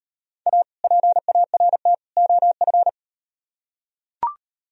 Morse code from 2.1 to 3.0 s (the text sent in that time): OF